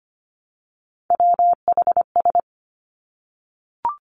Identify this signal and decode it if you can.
Morse code, heard as a keyed tone, about 25 wpm, 700 Hz. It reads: WHS
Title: Three Challenging Letter-Number Combinations 25wpm